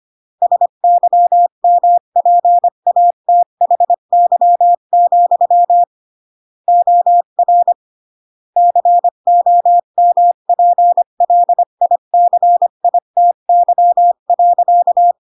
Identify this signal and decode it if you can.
Morse code, heard as a keyed tone, about 25 wpm, 700 Hz. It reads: SYMPATHY, OR COMPLICITY.